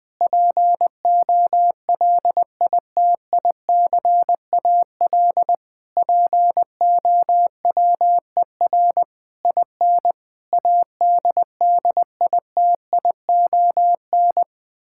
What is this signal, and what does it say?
Morse code, heard as a keyed tone, about 20 wpm, 700 Hz. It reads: POLITICAL POWER IN ADDITION
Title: Sentences from Top 1000 Words - Review Part 6 - 20wpm